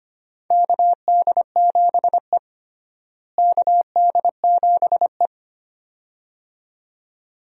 Morse code, heard as a keyed tone, about 25 wpm, 700 Hz. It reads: KD7E KD7E
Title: USA Call Signs - 4 Characters - Repeated 2x - 25wpm